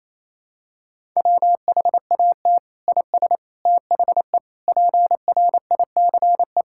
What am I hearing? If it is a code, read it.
Morse code, heard as a keyed tone, about 28 wpm, 700 Hz. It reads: WHAT IS THE PRICE
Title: Sentences from Top 1000 Words - Review Part 6 - 28wpm